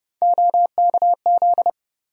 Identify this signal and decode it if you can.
Morse code, heard as a keyed tone, about 30 wpm, 700 Hz. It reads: OKZ
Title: USA Call Sign Postfixes - 3 Letters 30wpm